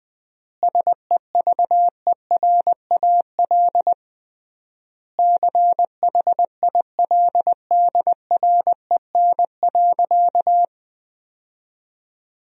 Morse code, heard as a keyed tone, about 20 wpm, 700 Hz. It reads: SEVERAL CHILDREN.
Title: 3x Word-Spacing - Sentences from Top 400 Words - 20wpm